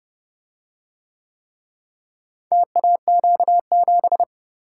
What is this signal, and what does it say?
Morse code, heard as a keyed tone, about 30 wpm, 700 Hz. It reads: TAQ7